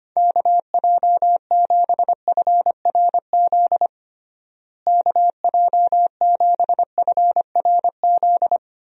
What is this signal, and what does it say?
Morse code, heard as a keyed tone, about 25 wpm, 700 Hz. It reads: KJ7FRZ KJ7FRZ
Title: USA Call Signs - 6 Characters - Repeated 3x - 25wpm